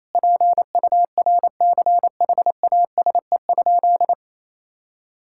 Morse code, heard as a keyed tone, about 28 wpm, 700 Hz. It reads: PURCHASE?
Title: Sentences from Top 1000 Words - Review Part 6 - 28wpm